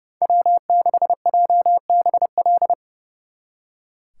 Morse code, heard as a keyed tone, about 36 wpm, 700 Hz. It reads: W6JBL
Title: Speed Racing - USA 1x3 Call Signs 30wpm